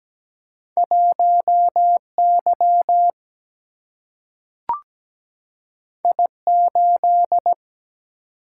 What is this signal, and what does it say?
Morse code, heard as a keyed tone, about 17 wpm, 700 Hz. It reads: 1Y I8
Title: Single-Letter Single-Number 17wpm